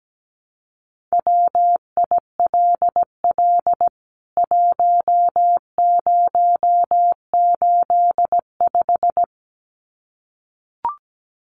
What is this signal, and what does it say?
Morse code, heard as a keyed tone, about 17 wpm, 700 Hz. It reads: WILL 1085
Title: CWT Exchange 17wpm